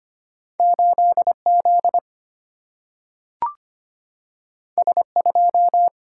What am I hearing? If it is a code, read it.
Morse code, heard as a keyed tone, about 25 wpm, 700 Hz. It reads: 8 Z S 2
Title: Single-Letter Single-Number 25wpm